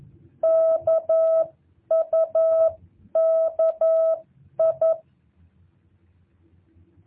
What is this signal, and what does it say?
Morse code, heard as a keyed tone, about 11 wpm, spaced as for 9 wpm, 650 Hz. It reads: KUKI
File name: jh8cbh / 2023cw